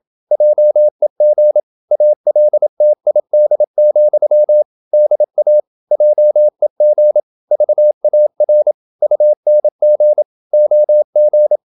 Morse code, heard as a keyed tone, about 27 wpm, 600 Hz. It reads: JEG ALTID, DA JEG VAR UNG OG
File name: Ferie QRQ 27wpm